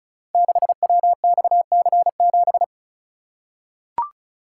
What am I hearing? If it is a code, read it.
Morse code, heard as a keyed tone, about 36 wpm, 690 Hz. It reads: 6WXC7
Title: Code Groups - 5 Letter:Numbers - 35wpm